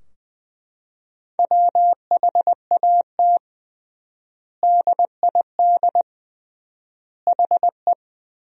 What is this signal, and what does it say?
Morse code, heard as a keyed tone, about 20 wpm, 700 Hz. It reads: WHAT DID HE
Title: 3x Word-Spacing - Sentences from Top 300 Words - 20wpm